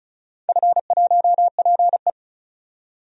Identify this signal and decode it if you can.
Morse code, heard as a keyed tone, about 36 wpm, 690 Hz. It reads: F1PE
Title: DXCC-100 Call Signs - 4 Characters - 35wpm